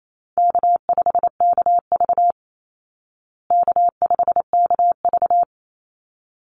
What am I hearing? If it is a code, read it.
Morse code, heard as a keyed tone, about 28 wpm, 700 Hz. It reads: K5KV K5KV
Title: Easy Call Signs Repeated 2x - 28wpm